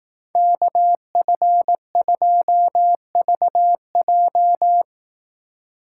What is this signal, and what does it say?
Morse code, heard as a keyed tone, about 18 wpm, 700 Hz. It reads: KF2VJ
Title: Speed Racing - USA 2x2 Call Signs 15wpm